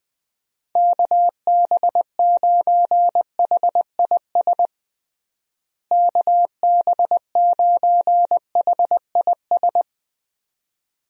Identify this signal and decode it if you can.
Morse code, heard as a keyed tone, about 20 wpm, 700 Hz. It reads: KB9HIS KB9HIS
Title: USA Call Signs - 6 Characters - Repeated 2x - 20wpm